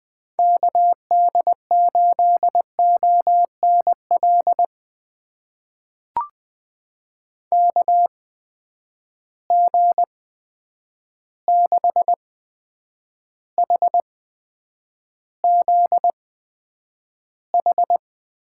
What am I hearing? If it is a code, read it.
Morse code, heard as a keyed tone, about 20 wpm, spaced as for 9 wpm, 700 Hz. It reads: KD8ONL KG6HZH
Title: USA Call Signs - 6 Characters - 8x Character Spacing - 20wpm